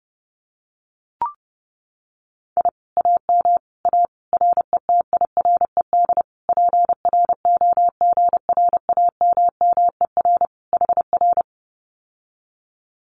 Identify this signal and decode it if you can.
Morse code, heard as a keyed tone, about 30 wpm, 700 Hz. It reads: I AM A RETIRED PROGRAMMER HR
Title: Common QSO Phrases Encore 30wpm